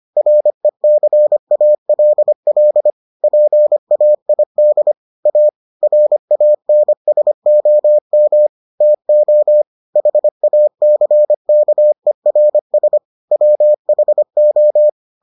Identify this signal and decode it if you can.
Morse code, heard as a keyed tone, about 25 wpm, 600 Hz. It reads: RECALL PAID A RANSOM TO HACKERS WHO